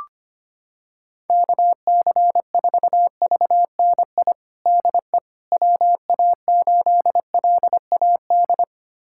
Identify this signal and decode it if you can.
Morse code, heard as a keyed tone, about 25 wpm, 700 Hz. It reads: KC4VNI DE WA8LAD
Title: USA Callsign DE Callsign 25wpm